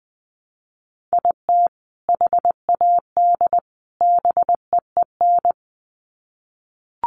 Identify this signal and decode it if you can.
Morse code, heard as a keyed tone, about 20 wpm, 700 Hz. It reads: IT HAD BEEN
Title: Sets of 3 Words 20wpm